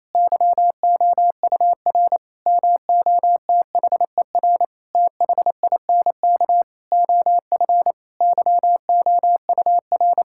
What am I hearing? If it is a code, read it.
Morse code, heard as a keyed tone, about 28 wpm, 700 Hz. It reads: YOUR MOTHER THINK OF YOUR
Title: Sentences from Top 1000 Words - Review Part 5 - 28wpm